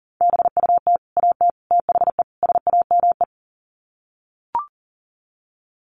Morse code, heard as a keyed tone, about 40 wpm, 700 Hz. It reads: BUT AT THE SAME
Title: Sets of 4 Words 40wpm